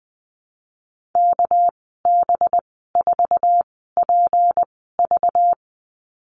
Morse code, heard as a keyed tone, about 20 wpm, spaced as for 15 wpm, 700 Hz. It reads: KB4PV
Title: 207 Lesson 33g - Letter j - Call Signs 20wpm